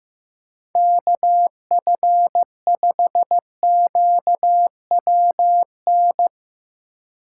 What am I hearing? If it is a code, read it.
Morse code, heard as a keyed tone, about 15 wpm, 700 Hz. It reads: KF5QWN
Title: USA Call Signs - 6 Characters - 2x Character Spacing - 15wpm